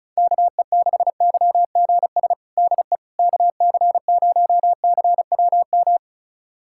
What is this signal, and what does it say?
Morse code, heard as a keyed tone, about 36 wpm, 700 Hz. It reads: KE6YGS DE KC0CWM
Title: USA Callsign DE Callsign 35wpm